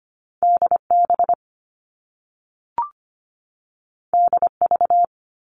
Morse code, heard as a keyed tone, about 25 wpm, 700 Hz. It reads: DB DV
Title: All Cut Numbers - 2-Digit 25wpm